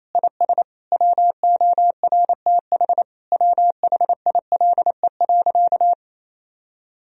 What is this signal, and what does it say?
Morse code, heard as a keyed tone, about 28 wpm, 700 Hz. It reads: IS WORTH WHILE.
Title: Sentences from Top 1000 Words - Review Part 1 - 28wpm